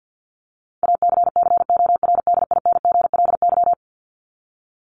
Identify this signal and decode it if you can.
Morse code, heard as a keyed tone, about 50 wpm, 700 Hz. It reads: ACCORDINGLY
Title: LY Suffix - Top 100 Words with Base Words 50wpm